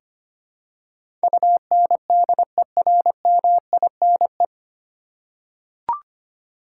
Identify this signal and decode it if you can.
Morse code, heard as a keyed tone, about 25 wpm, 700 Hz. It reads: UNDERMINE
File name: practice / 25wpm / 9 Letter Words - First 3 Letters Given - 25wpm